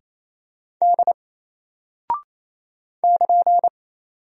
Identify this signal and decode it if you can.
Morse code, heard as a keyed tone, about 27 wpm, 700 Hz. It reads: D <KN>
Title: Single Character (Letter:Number:Punctuation:Prosign) 28wpm